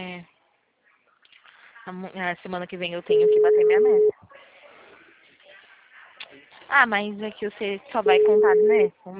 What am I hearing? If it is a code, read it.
Ringback tone (Europe, 425 Hz), with speech.